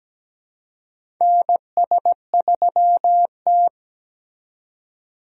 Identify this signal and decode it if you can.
Morse code, heard as a keyed tone, about 17 wpm, 700 Hz. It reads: NS3T
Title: USA Call Signs - 2x Character Spacing - 17wpm